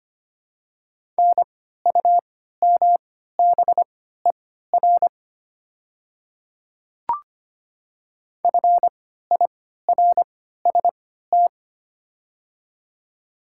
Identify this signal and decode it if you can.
Morse code, heard as a keyed tone, about 25 wpm, spaced as for 14 wpm, 700 Hz. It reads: NUMBER FIRST
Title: 3x Character Spacing - Top 200 Words - 25wpm